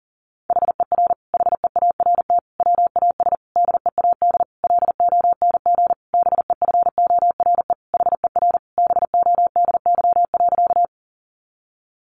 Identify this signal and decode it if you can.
Morse code, heard as a keyed tone, about 40 wpm, 700 Hz. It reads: HER HEART WAS DEAD LONG BEFORE HER BODY.
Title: Sentences from Top 1500 Words 40wpm